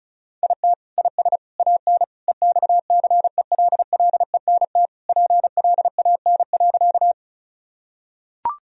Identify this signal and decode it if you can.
Morse code, heard as a keyed tone, about 36 wpm, 700 Hz. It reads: IT IS AN EXCELLENT PLAN.
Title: Sentences from Top 1000 Words - Review Part 2 - 35wpm